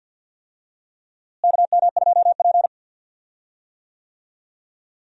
Morse code, heard as a keyed tone, about 45 wpm, 700 Hz. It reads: KM2P
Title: USA Call Signs - 4 Characters - Repeated 3x - 50wpm